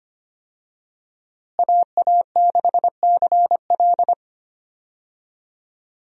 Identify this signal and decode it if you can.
Morse code, heard as a keyed tone, about 25 wpm, 700 Hz. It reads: AA6CL
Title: USA Call Signs - 5 Characters - Repeated 2x - 25wpm